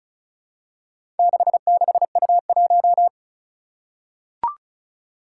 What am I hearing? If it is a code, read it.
Morse code, heard as a keyed tone, about 36 wpm, 700 Hz. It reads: 66U1
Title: Four Challenging Letter-Number Combinations 35wpm